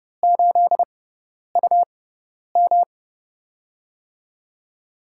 Morse code, heard as a keyed tone, about 30 wpm, spaced as for 10 wpm, 700 Hz. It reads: 8UM
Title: USA Call Signs - 4 Characters - 6x Character Spacing - 30wpm